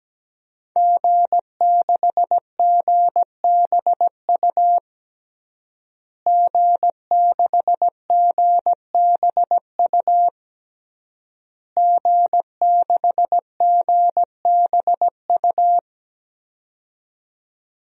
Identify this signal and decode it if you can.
Morse code, heard as a keyed tone, about 17 wpm, 700 Hz. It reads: G6GBU G6GBU G6GBU
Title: DXCC-100 Call Signs - 5 Characters - Repeated 3x - 17wpm